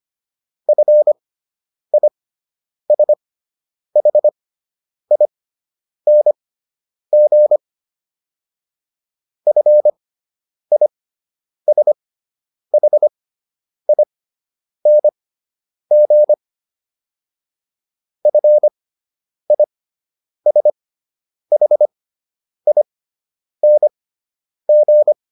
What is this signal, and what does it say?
Morse code, heard as a keyed tone, about 25 wpm, spaced as for 9 wpm, 600 Hz. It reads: FISHING FISHING FISHING